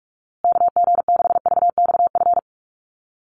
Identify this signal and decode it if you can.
Morse code, heard as a keyed tone, about 44 wpm, 700 Hz. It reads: KG6VXF
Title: USA Call Signs Repeated 2x 45wpm